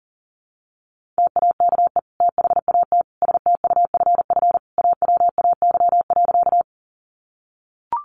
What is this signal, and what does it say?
Morse code, heard as a keyed tone, about 40 wpm, 700 Hz. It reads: TAKE THAT STUFF AWAY.
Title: Sentences from Top 1400 Words 40wpm